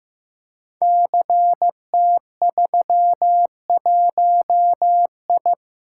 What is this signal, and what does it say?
Morse code, heard as a keyed tone, about 15 wpm, 700 Hz. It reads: CT31I
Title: Code Groups - 5 Letter:Numbers - 15wpm